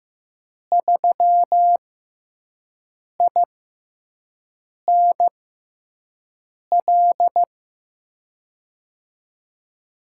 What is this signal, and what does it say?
Morse code, heard as a keyed tone, about 15 wpm, spaced as for 5 wpm, 700 Hz. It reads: 3INL
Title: USA Call Signs - 6 Characters - 6x Character Spacing - 15wpm